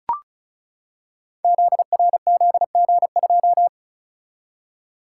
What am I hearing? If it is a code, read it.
Morse code, heard as a keyed tone, about 36 wpm, 700 Hz. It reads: ZRZG2
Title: Code Groups - 5 Letter:Numbers - 35wpm